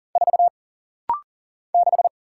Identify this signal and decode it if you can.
Morse code, heard as a keyed tone, about 40 wpm, 700 Hz. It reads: 4 6
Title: Single Letter-Number Alt - Rapid-Fire 40wpm